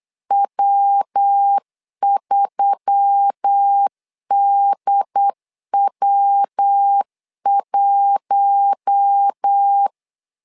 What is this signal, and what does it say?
Morse code, heard as a keyed tone, about 8 wpm, 800 Hz. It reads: W3DW1
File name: LEZIONI / 4 Q1S3 8wpm 3037